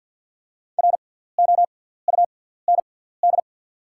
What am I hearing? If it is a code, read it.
Morse code, heard as a keyed tone, about 48 wpm, spaced as for 17 wpm, 690 Hz. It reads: ROUND
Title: 6x Character Spacing - Top 300 Words - 50wpm